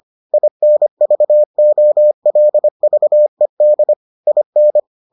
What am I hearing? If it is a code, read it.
Morse code, heard as a keyed tone, about 25 wpm, 600 Hz. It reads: INVOLVED IN